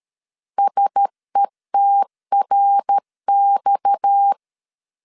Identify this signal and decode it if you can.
Morse code, heard as a keyed tone, about 13 wpm, 800 Hz. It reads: SETRX
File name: LEZIONI / 6 PKX0 / 6 PKX 12wpm 2024